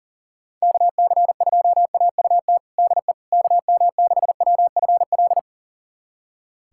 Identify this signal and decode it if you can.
Morse code, heard as a keyed tone, about 40 wpm, 700 Hz. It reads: KC2AUT DE KM6WFL
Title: USA Callsign DE Callsign - 6 Characters - 40wpm